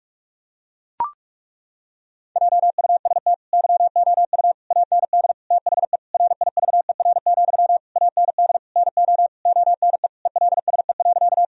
Morse code, heard as a keyed tone, about 41 wpm, 700 Hz. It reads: JUST YOU AND THE RIVER, AND NO ONE ELSE.
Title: Sentences from Top 1000 Words - Review Part 1 - 45wpm